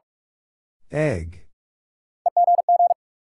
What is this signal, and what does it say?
Morse code, heard as a keyed tone, about 41 wpm, 700 Hz. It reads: EGG
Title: Sentences from Top 1000 Words - Review Part 7 - New Word Drill - 45wpm